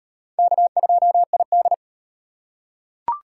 Morse code, heard as a keyed tone, about 35 wpm, 700 Hz. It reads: K2ID